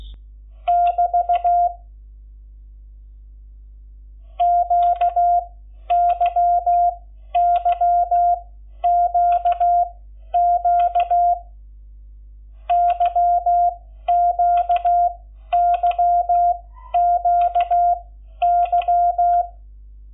Morse code, heard as a keyed tone, about 16 wpm, spaced as for 11 wpm, 700 Hz. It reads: <BT> QYYQQ YQYQY
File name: Lecciones / CLASE Nº11